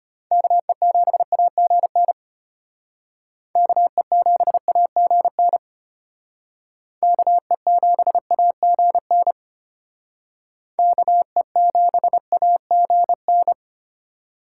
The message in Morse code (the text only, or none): KE7AGN KE7AGN KE7AGN KE7AGN